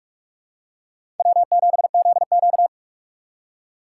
Morse code, heard as a keyed tone, about 46 wpm, 700 Hz. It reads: W7ZQ
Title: USA Call Signs - 4 Characters - 2x Character Spacing - 45wpm